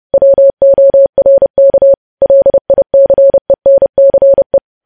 Morse code, heard as a keyed tone, about 30 wpm, 550 Hz. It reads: WORK LICENCE